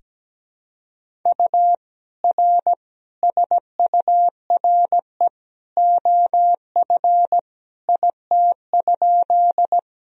Morse code, heard as a keyed tone, about 17 wpm, 700 Hz. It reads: U R SURE OF IT?
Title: Sentences from Top 300 Words and Abbreviations 17wpm